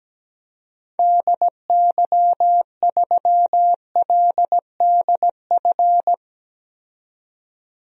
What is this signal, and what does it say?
Morse code, heard as a keyed tone, about 17 wpm, 700 Hz. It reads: DY3LDF
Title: DXCC-100 Call Signs - 6 Characters - 17wpm